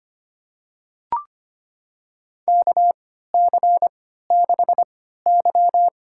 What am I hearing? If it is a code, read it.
Morse code, heard as a keyed tone, about 25 wpm, spaced as for 14 wpm, 700 Hz. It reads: KC6Y
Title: USA Call Signs - 3x Character Spacing - 25wpm